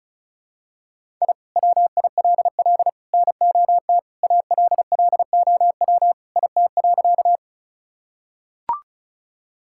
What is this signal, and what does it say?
Morse code, heard as a keyed tone, about 34 wpm, 700 Hz. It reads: I WILL NOT ALLOW IT.
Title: Sentences from Top 1000 Words - Review Part 5 - 35wpm